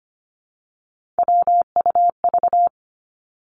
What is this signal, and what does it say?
Morse code, heard as a keyed tone, about 25 wpm, 700 Hz. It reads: WUV
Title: Three Challenging Letter-Number Combinations 25wpm